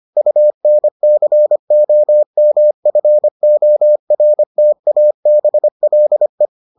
Morse code, heard as a keyed tone, about 25 wpm, 600 Hz. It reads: UNCOMFORTABLE